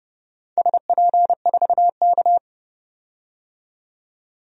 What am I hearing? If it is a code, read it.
Morse code, heard as a keyed tone, about 30 wpm, 700 Hz. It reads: SP4K
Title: DXCC-100 Call Signs - 4 Characters - 30wpm